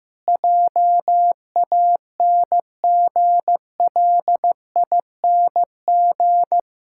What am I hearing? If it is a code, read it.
Morse code, heard as a keyed tone, about 15 wpm, 700 Hz. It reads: JANGLING
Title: Pangram Challenge v2 - 15wpm